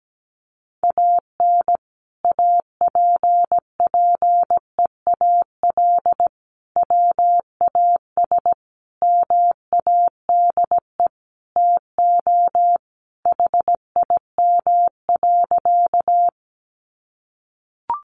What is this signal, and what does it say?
Morse code, heard as a keyed tone, about 17 wpm, 700 Hz. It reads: AN APPEAL WAS MADE TO HIM.
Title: Sentences from Top 1300 Words 17wpm